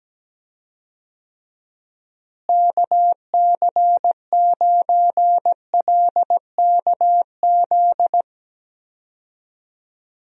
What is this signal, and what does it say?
Morse code, heard as a keyed tone, about 17 wpm, 700 Hz. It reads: KC9LKZ
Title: USA Call Signs - 6 Characters - Repeated 2x - 17wpm